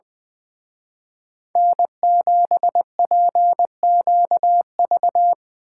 Morse code, heard as a keyed tone, about 24 wpm, 700 Hz. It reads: N7PQV
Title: Speed Racing - USA Call Signs 20wpm